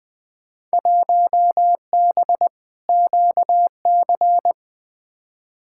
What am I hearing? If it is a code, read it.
Morse code, heard as a keyed tone, about 20 wpm, 700 Hz. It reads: 1B QC
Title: Field Day Exchanges Encore - 20wpm